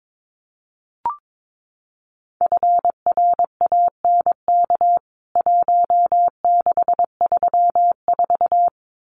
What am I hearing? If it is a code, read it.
Morse code, heard as a keyed tone, about 22 wpm, 700 Hz. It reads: FRANK 1634
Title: CWT Exchange 22wpm